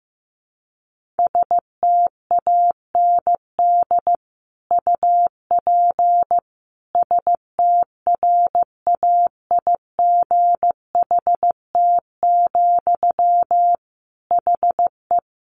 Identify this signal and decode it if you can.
Morse code, heard as a keyed tone, about 15 wpm, 700 Hz. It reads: STAND UP STRAIGHT, HE
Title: Sentences from Top 1000 Words 15wpm